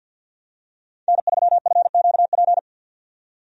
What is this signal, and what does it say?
Morse code, heard as a keyed tone, about 53 wpm, 700 Hz. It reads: N3FQP
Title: USA Call Signs - 5 Characters - 50wpm